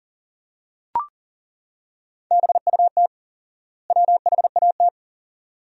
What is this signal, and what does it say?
Morse code, heard as a keyed tone, about 40 wpm, 700 Hz. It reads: BUT WHAT
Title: 4x Word-Spacing - Sentences from Top 100 Words - 40wpm